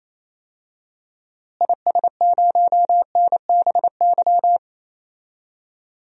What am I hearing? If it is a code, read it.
Morse code, heard as a keyed tone, about 28 wpm, 700 Hz. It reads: IS0NBY
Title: DXCC-100 Call Signs - 6 Characters - Repeated 3x - 28wpm